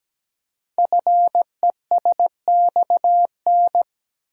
Morse code, heard as a keyed tone, about 17 wpm, 700 Hz. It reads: FESXN
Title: Code Groups - 5 Letters - 17wpm